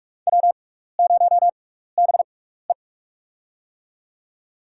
Morse code, heard as a keyed tone, about 43 wpm, spaced as for 15 wpm, 700 Hz. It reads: W0BE
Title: USA Call Signs - 4 Characters - 6x Character Spacing - 45wpm